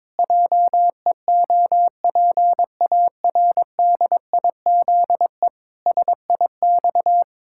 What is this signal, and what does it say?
Morse code, heard as a keyed tone, about 22 wpm, 700 Hz. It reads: JEOPARDIZE SIX
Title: Pangram Challenge v2 - 22wpm